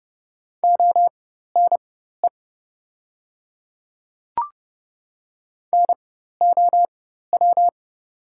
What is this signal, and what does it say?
Morse code, heard as a keyed tone, about 30 wpm, spaced as for 14 wpm, 700 Hz. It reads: ONE NOW
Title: 4x Character Spacing - Top 100 Words - 30wpm